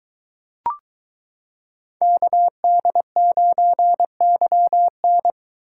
Morse code, heard as a keyed tone, about 23 wpm, 700 Hz. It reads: KD9YN